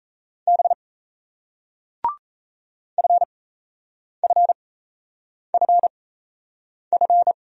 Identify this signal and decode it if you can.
Morse code, heard as a keyed tone, about 38 wpm, spaced as for 8 wpm, 700 Hz. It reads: B FFFF